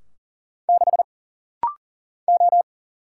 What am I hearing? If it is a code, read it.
Morse code, heard as a keyed tone, about 40 wpm, 700 Hz. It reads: 6 O
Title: Single Letter-Number Alt - Rapid-Fire 40wpm